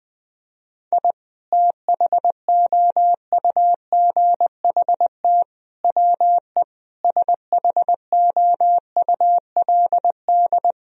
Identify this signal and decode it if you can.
Morse code, heard as a keyed tone, about 20 wpm, 700 Hz. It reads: I THOUGHT WE SHOULD
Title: Sentences from Top 1000 Words - Review Part 2 - 20wpm